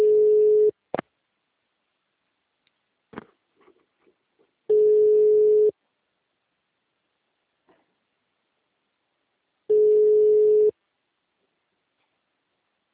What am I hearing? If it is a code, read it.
Ringback tone (Europe, 425 Hz).
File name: out- -2012- - - .2132